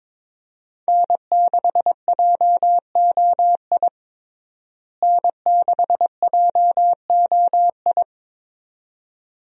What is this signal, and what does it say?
Morse code, heard as a keyed tone, about 21 wpm, 700 Hz. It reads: N6JOI N6JOI
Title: USA Call Signs - 5 Characters - Repeated 2x - 22wpm